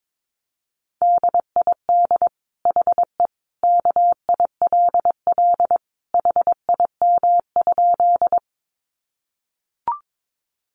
Morse code, heard as a keyed tone, about 22 wpm, 700 Hz. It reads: DID HE KILL HIM?
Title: Sentences from Top 1400 Words 22wpm